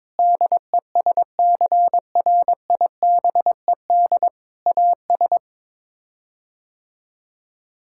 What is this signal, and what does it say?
Morse code, heard as a keyed tone, about 22 wpm, 700 Hz. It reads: DESCRIBED AS
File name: practice / 22wpm / Sets of 6 Words - 22wpm